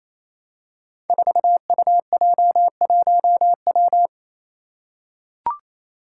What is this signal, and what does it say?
Morse code, heard as a keyed tone, about 27 wpm, 700 Hz. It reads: 4UJ1W
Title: Five Challenging Letter-Number Combinations 28wpm